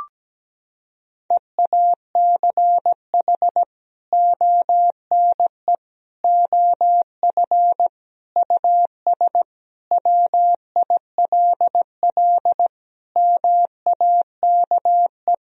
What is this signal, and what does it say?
Morse code, heard as a keyed tone, about 17 wpm, 700 Hz. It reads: EACH ONE OF US WILL MAKE
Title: Sentences from Top 1000 Words - Review Part 2 - 17wpm